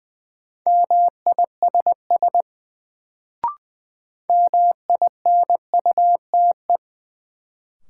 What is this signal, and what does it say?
Morse code, heard as a keyed tone, about 20 wpm, 700 Hz. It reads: MISS MINUTE
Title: Sentences from Top 500 Words - New Word Drill - 20wpm